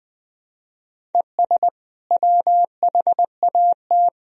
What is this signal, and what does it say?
Morse code, heard as a keyed tone, about 21 wpm, 700 Hz. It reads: ES WHAT
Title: Sentences from Top 200 Words and Abbreviations 20wpm